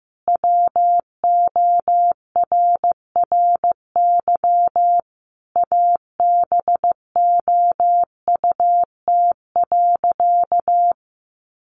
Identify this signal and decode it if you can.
Morse code, heard as a keyed tone, about 15 wpm, 700 Hz. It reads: WORRY ABOUT.
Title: Sentences from Top 1100 Words 15wpm